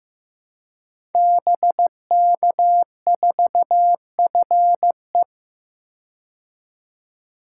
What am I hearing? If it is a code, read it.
Morse code, heard as a keyed tone, about 15 wpm, 700 Hz. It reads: BK4FE
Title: Code Groups - 5 Letter:Numbers - 15wpm